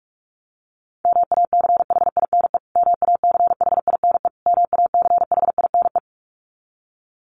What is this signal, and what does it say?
Morse code, heard as a keyed tone, about 46 wpm, 700 Hz. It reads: MACHINE MACHINE MACHINE
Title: Top 500 Words Repeated 45wpm